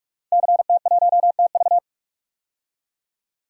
Morse code, heard as a keyed tone, about 44 wpm, 700 Hz. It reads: CT1TV
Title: DXCC-100 Call Signs - 5 Characters - 45wpm